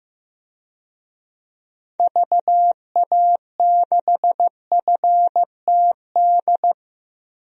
Morse code, heard as a keyed tone, about 15 wpm, 700 Hz. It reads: VA6FTD
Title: DXCC-100 Call Signs - 6 Characters - Repeated 3x - 15wpm